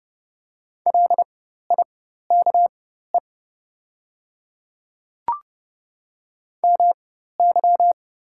Morse code, heard as a keyed tone, about 30 wpm, spaced as for 14 wpm, 690 Hz. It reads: LIKE MY
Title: 4x Character Spacing - Top 100 Words - 30wpm